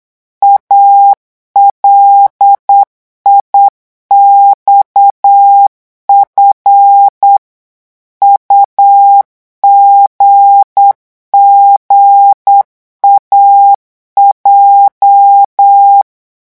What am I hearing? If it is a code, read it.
Morse code, heard as a keyed tone, about 8 wpm, 800 Hz. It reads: ALIXF UGGAJ
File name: morse mp3 / revision8-vitesse 8 mots